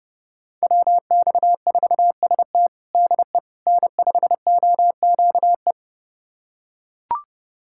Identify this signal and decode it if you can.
Morse code, heard as a keyed tone, about 30 wpm, 700 Hz. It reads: WX4ST DE N5OQE
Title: USA Callsign DE Callsign 30wpm